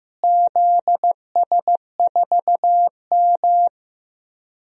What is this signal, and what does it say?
Morse code, heard as a keyed tone, about 16 wpm, 700 Hz. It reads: ZS4M
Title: DXCC-100 Call Signs - 4 Characters - Repeated 2x - 15wpm